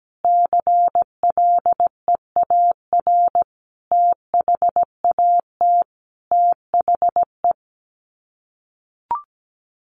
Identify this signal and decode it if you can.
Morse code, heard as a keyed tone, about 17 wpm, 700 Hz. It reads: CLEAR THAT THE
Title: Sets of 5 Words Encore 17wpm